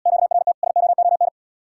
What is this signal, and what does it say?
Morse code, heard as a keyed tone, about 78 wpm, 700 Hz. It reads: QRT SOON